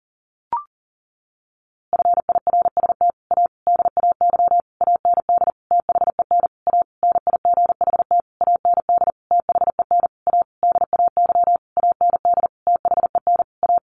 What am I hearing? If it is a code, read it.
Morse code, heard as a keyed tone, about 40 wpm, 700 Hz. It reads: FIRST A DAY AND THEN A NIGHT AND THEN A DAY AND THEN A
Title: A Sound Of Thunder 40wpm